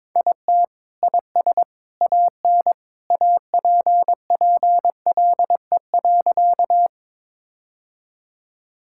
Morse code, heard as a keyed tone, about 22 wpm, 700 Hz. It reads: IT IS AN APPLE.
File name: practice / 22wpm / Sentences with 2-Word Phrases - 22wpm